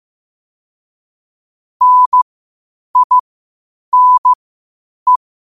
Morse code, heard as a keyed tone, about 15 wpm, spaced as for 8 wpm, 1000 Hz. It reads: NINE
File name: qotdfiles / 8wpm